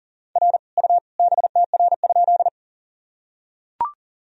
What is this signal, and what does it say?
Morse code, heard as a keyed tone, about 38 wpm, 700 Hz. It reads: R U BTR?
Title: Sentences from Top 400 Words and Abbreviations 40wpm